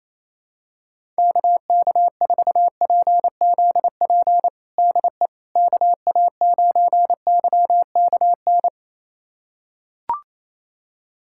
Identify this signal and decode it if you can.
Morse code, heard as a keyed tone, about 28 wpm, 700 Hz. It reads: KK4PZP DE KA9YKN
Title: USA Callsign DE Callsign - 6 Characters - 28wpm